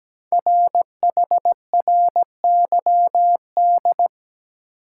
Morse code, heard as a keyed tone, about 17 wpm, 700 Hz. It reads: RHRYD
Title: Code Groups - 5 Letters - 17wpm